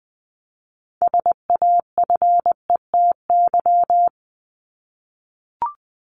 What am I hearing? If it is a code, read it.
Morse code, heard as a keyed tone, about 20 wpm, 700 Hz. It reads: SAFETY
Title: Top 501-750 Words 20wpm